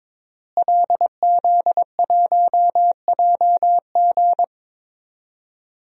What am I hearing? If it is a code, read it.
Morse code, heard as a keyed tone, about 22 wpm, 700 Hz. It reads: LZ1JG
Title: DXCC-100 Call Signs - 5 Characters - Repeated 2x - 22wpm